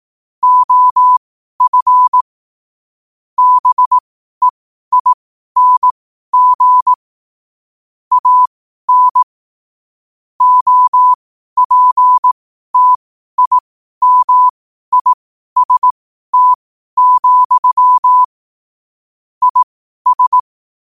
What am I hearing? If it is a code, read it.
Morse code, heard as a keyed tone, about 18 wpm, spaced as for 12 wpm, 1000 Hz. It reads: OF BEING AN OPTIMIST, IS
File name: qotdfiles / 12wpm